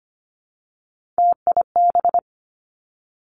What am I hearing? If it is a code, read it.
Morse code, heard as a keyed tone, about 25 wpm, 700 Hz. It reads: TIB
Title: Three Challenging Letter-Number Combinations 25wpm